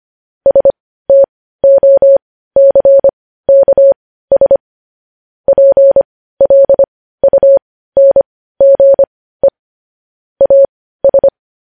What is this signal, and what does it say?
Morse code, heard as a keyed tone, about 25 wpm, spaced as for 15 wpm, 550 Hz. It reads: STOCKS PLUNGE AS
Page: news headlines in morse code at 15 wpm